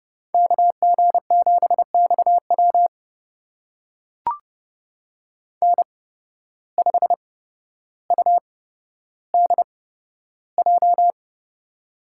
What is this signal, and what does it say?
Morse code, heard as a keyed tone, about 30 wpm, spaced as for 13 wpm, 700 Hz. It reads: KG7XW N5UDJ
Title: USA Call Signs - 5 Characters - 8x Character Spacing - 30wpm